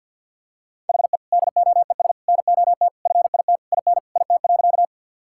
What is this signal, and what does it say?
Morse code, heard as a keyed tone, about 48 wpm, 700 Hz. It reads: HE DOES NOT FIT IN IT.
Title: Sentences from Top 1000 Words - Review Part 5 - 50wpm